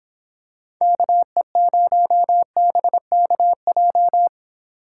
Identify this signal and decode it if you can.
Morse code, heard as a keyed tone, about 26 wpm, 700 Hz. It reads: KE0BKJ